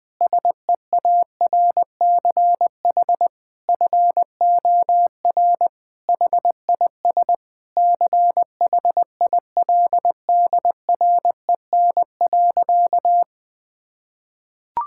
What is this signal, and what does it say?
Morse code, heard as a keyed tone, about 20 wpm, 700 Hz. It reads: SEARCH FOR HIS CHILDREN.
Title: Sentences from Top 1000 Words - Review Part 5 - 20wpm